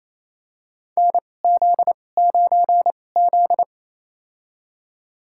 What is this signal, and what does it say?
Morse code, heard as a keyed tone, about 28 wpm, spaced as for 20 wpm, 700 Hz. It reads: NZ9Z
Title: USA Call Signs - 2x Character Spacing - 28wpm